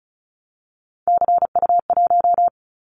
Morse code, heard as a keyed tone, about 36 wpm, 700 Hz. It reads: CU1
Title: Callsign Prefixes for Top 50 DXCC - 35wpm